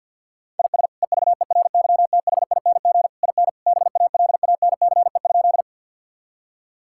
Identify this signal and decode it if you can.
Morse code, heard as a keyed tone, about 50 wpm, 700 Hz. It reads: IS EVERYTHING IN BALANCE?
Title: Sentences from Top 1000 Words - Review Part 6 - 50wpm